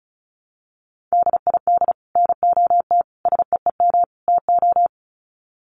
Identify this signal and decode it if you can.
Morse code, heard as a keyed tone, about 34 wpm, 700 Hz. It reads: DID NOT SEEM TO
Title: Sets of 4 Words 35wpm